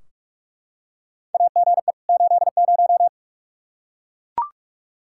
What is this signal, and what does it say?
Morse code, heard as a keyed tone, about 43 wpm, 700 Hz. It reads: AGE 80
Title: Practice QSO - Exchange 3 - More Info - 45wpm